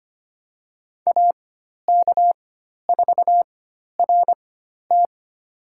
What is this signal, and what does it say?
Morse code, heard as a keyed tone, about 25 wpm, spaced as for 12 wpm, 700 Hz. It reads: AK4RT
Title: USA Call Signs - 5 Characters - 4x Character Spacing - 25wpm